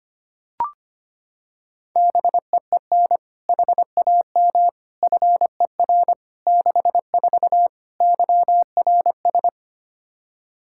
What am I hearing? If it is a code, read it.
Morse code, heard as a keyed tone, about 25 wpm, 700 Hz. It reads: BEEN HAM FER 64 YRS
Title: K1USN SST QSO Practice 25wpm